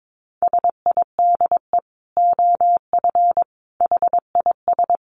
Morse code, heard as a keyed tone, about 22 wpm, 700 Hz. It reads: SIDE OF HIS
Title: Dit Challenge 22wpm